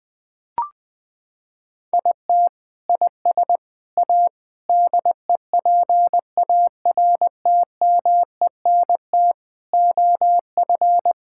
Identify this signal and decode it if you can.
Morse code, heard as a keyed tone, about 20 wpm, 700 Hz. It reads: IT IS A DEPARTMENT OF
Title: Sentences from Top 1000 Words - Review Part 5 - 20wpm